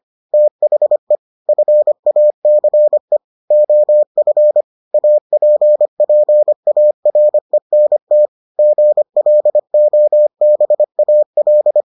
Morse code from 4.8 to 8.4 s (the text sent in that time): APPARENT